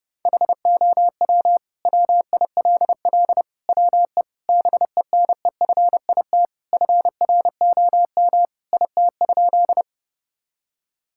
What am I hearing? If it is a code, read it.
Morse code, heard as a keyed tone, about 32 wpm, 700 Hz. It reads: HOW WILL WE BENEFIT FROM IT?
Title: Sentences from Top 1000 Words - Review Part 6 - 30wpm